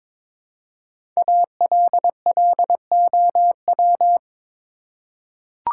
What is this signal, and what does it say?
Morse code, heard as a keyed tone, about 22 wpm, 700 Hz. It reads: ALLOW
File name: 5 Letter Words - First 3 Letters Given - 22wpm